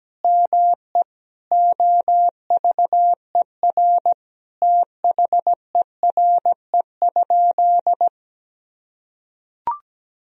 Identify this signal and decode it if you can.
Morse code, heard as a keyed tone, about 17 wpm, 700 Hz. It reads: ME OVER THERE?
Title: Sentences from Top 1000 Words - Review Part 3 - 17wpm